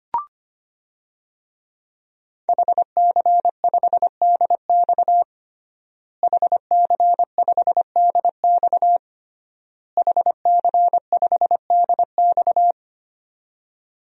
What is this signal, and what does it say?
Morse code, heard as a keyed tone, about 25 wpm, 700 Hz. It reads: HC5DX HC5DX HC5DX
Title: DXCC-100 Call Signs - 5 Characters - Repeated 3x - 25wpm